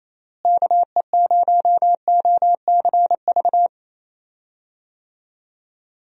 Morse code, heard as a keyed tone, about 28 wpm, 700 Hz. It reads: KE0OCV
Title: USA Call Signs - 6 Characters - Repeated 2x - 28wpm